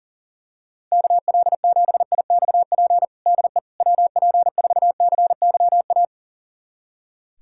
Morse code, heard as a keyed tone, about 40 wpm, 700 Hz. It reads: KL7IXP DE WP4CYA
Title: USA Callsign DE Callsign - 6 Characters - 40wpm